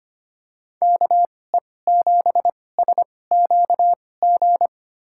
Morse code, heard as a keyed tone, about 25 wpm, spaced as for 18 wpm, 700 Hz. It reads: KE7SQG
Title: USA Call Signs - 6 Characters - 2x Character Spacing - 25wpm